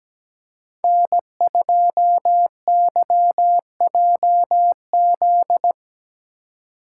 Morse code, heard as a keyed tone, about 20 wpm, 700 Hz. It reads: N2YJZ
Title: Speed Racing - USA 1x3 Call Signs 17wpm